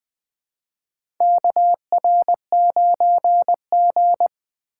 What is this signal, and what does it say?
Morse code, heard as a keyed tone, about 20 wpm, 700 Hz. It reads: KR9G
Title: USA Call Signs - 3x Character Spacing - 20wpm